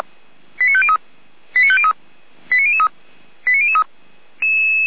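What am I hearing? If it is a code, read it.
Five-tone selective call (ZVEI), heard as quick runs of five tones, tones of about 70 ms. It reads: 89503 89503 89003 89003
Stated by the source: ZVEI five-tone selective call received by SDR.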